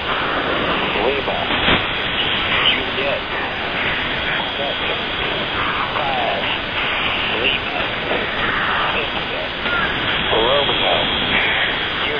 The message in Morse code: none

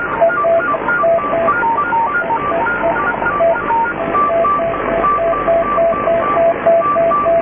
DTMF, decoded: none

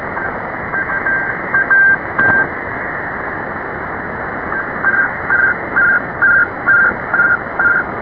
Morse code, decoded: none

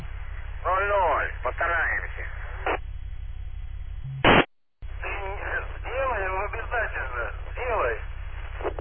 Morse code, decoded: none

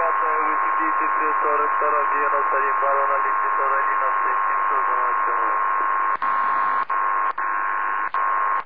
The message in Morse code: none

